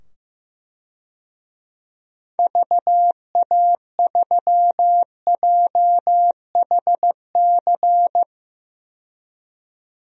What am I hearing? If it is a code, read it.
Morse code, heard as a keyed tone, about 15 wpm, 700 Hz. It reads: VA3JHC
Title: DXCC-100 Call Signs - 6 Characters - 15wpm